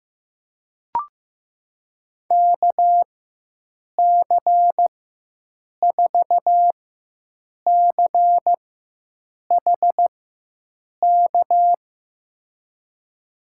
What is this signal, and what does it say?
Morse code, heard as a keyed tone, about 15 wpm, spaced as for 7 wpm, 700 Hz. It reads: KC4CHK
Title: USA Call Signs - 6 Characters - 4x Character Spacing - 15wpm